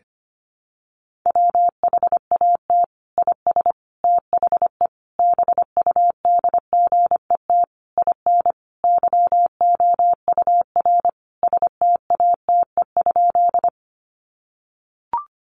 Morse code, heard as a keyed tone, about 25 wpm, 700 Hz. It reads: WHAT IS THE BUDGET IN YOUR STATE?
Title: Sentences from Top 1200 Words 25wpm